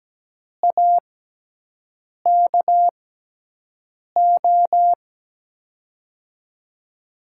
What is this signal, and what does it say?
Morse code, heard as a keyed tone, about 17 wpm, spaced as for 6 wpm, 700 Hz. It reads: AKO
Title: USA Call Signs - 6 Characters - 6x Character Spacing - 17wpm